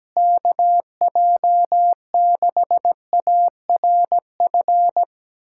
Morse code, heard as a keyed tone, about 17 wpm, 700 Hz. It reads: KJ6ARF
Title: USA Callsign DE Callsign - 6 Characters - 17wpm